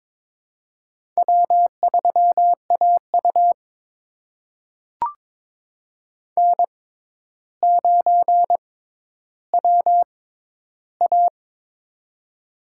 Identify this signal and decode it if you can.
Morse code, heard as a keyed tone, about 22 wpm, spaced as for 11 wpm, 700 Hz. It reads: W3AU N9WA
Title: USA Call Signs - 4 Characters - 6x Character Spacing - 22wpm